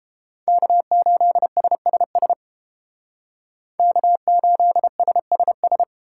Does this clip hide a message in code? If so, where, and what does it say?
Yes: Morse code from 0.5 to 5.8 s, heard as a keyed tone, about 29 wpm, 700 Hz. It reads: K8SSS K8SSS